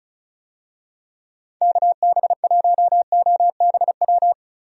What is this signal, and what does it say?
Morse code, heard as a keyed tone, about 34 wpm, 700 Hz. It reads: KB1OBW
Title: USA Call Signs - 6 Characters - Repeated 3x - 35wpm